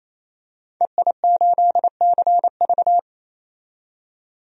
Morse code, heard as a keyed tone, about 28 wpm, 700 Hz. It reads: EI8CV
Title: DXCC-100 Call Signs - 5 Characters - 28wpm